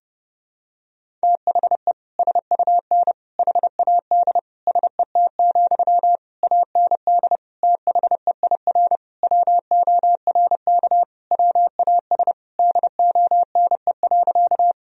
Morse code, heard as a keyed tone, about 32 wpm, 700 Hz. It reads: THE SUN HAD SET, AND THEIR WORK WAS DONE.
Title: Sentences from Top 1000 Words - Review Part 1 - 30wpm